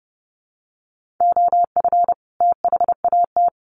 Morse code, heard as a keyed tone, about 30 wpm, 700 Hz. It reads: OF THAT
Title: Sets of 2 Words 30wpm